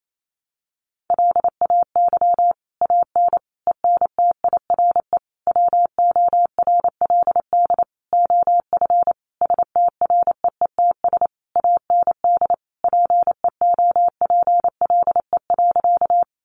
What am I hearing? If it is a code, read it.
Morse code, heard as a keyed tone, about 28 wpm, 700 Hz. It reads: LAY AN ENTIRE WORLD OF STREETS AND PEOPLE.
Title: A Sound Of Thunder 28wpm